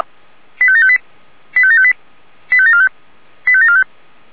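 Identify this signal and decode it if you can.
Five-tone selective call (ZVEI), heard as quick runs of five tones, tones of about 80 ms. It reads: 85758 85758 85745 85745
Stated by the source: ZVEI five-tone selective call 85735, received by SDR.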